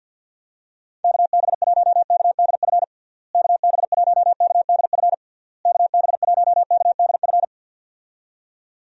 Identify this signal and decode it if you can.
Morse code, heard as a keyed tone, about 52 wpm, 700 Hz. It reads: KB1KDF KB1KDF KB1KDF
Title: USA Call Signs - 6 Characters - Repeated 3x - 50wpm